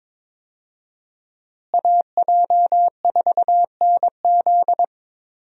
Morse code, heard as a keyed tone, about 22 wpm, 700 Hz. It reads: AJ4NZ
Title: USA Call Signs - 5 Characters - Repeated 2x - 22wpm